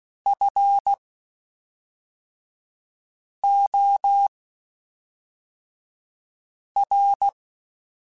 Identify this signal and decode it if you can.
Morse code, heard as a keyed tone, about 16 wpm, spaced as for 3 wpm, 800 Hz. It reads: FOR
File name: (spaced) / QSOs Spaced-mp3s / kc7haa @5 sp